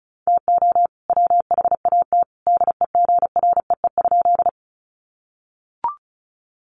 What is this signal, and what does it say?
Morse code, heard as a keyed tone, about 34 wpm, 700 Hz. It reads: TO WHAT DEGREE?
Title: Sentences from Top 1500 Words 35wpm